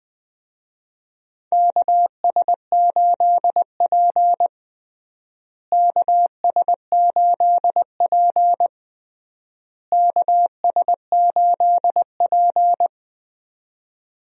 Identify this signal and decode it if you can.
Morse code, heard as a keyed tone, about 21 wpm, 700 Hz. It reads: KS8P KS8P KS8P
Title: USA Call Signs - 4 Characters - Repeated 3x - 20wpm